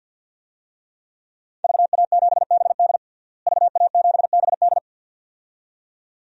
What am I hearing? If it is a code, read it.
Morse code, heard as a keyed tone, about 53 wpm, 700 Hz. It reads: VA7BD VA7BD
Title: DXCC-100 Call Signs - 5 Characters - Repeated 2x - 50wpm